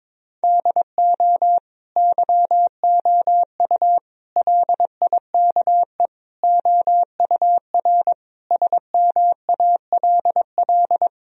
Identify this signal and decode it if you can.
Morse code, heard as a keyed tone, about 22 wpm, 700 Hz. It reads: DO YOU LIKE OUR SMALL
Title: Sentences from Top 1000 Words - Review Part 1 - 22wpm